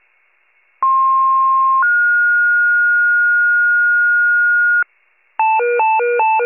Two-tone page, heard as a one-second tone, then a three-second tone, about 1060 then 1430 Hz, with noise. It is heard from 0.8 to 4.8 s.